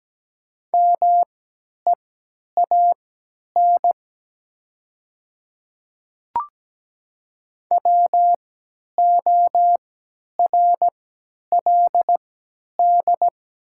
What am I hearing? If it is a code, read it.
Morse code, heard as a keyed tone, about 17 wpm, spaced as for 10 wpm, 700 Hz. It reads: MEAN WORLD
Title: 3x Character Spacing - Top 200 Words - 17wpm